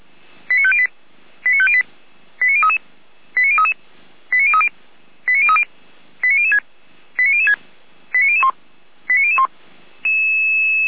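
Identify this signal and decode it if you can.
Five-tone selective call (ZVEI), heard as quick runs of five tones, tones of about 70 ms. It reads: 89508 89508 89033 89033 89030 89030 89006 89006 89001 89001
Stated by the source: ZVEI five-tone selective call received by SDR, recorded by Feuerwehr Orlamünde.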